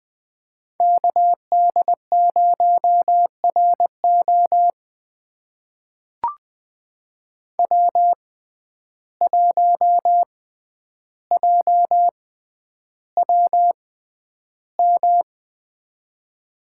Morse code, heard as a keyed tone, about 20 wpm, spaced as for 10 wpm, 700 Hz. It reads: KD0RO W1JWM
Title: USA Call Signs - 5 Characters - 6x Character Spacing - 20wpm